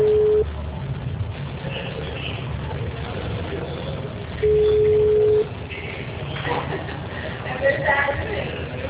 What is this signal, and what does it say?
Ringback tone (Europe, 425 Hz), with speech.